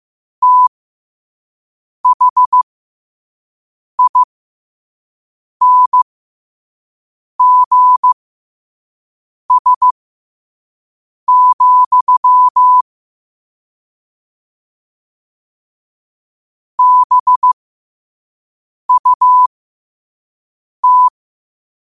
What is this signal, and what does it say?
Morse code, heard as a keyed tone, about 15 wpm, spaced as for 5 wpm, 1000 Hz. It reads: THINGS, BUT
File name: qotdfiles / 5wpm